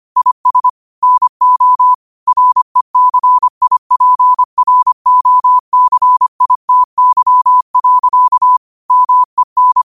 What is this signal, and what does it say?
Morse code, heard as a keyed tone, about 25 wpm, 1000 Hz. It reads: IS NO RECIPROCITY. MEN
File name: qotdfiles / 25wpm